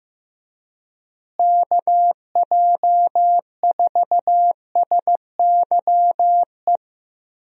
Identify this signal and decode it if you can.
Morse code, heard as a keyed tone, about 15 wpm, 700 Hz. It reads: KJ4SYE
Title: USA Call Signs - 6 Characters - Repeated 2x - 15wpm